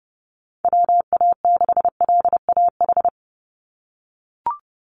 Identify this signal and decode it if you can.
Morse code, heard as a keyed tone, about 30 wpm, 700 Hz. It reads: WA6LAH
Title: USA Call Signs 30wpm